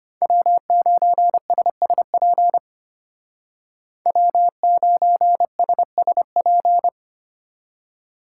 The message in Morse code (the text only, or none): W9SSP W9SSP